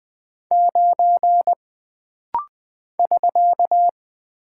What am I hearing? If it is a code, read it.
Morse code, heard as a keyed tone, about 20 wpm, 700 Hz. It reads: 9 <SK>
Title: Single Character (Letter:Number:Punctuation:Prosign) - Rapid-Fire - 20wpm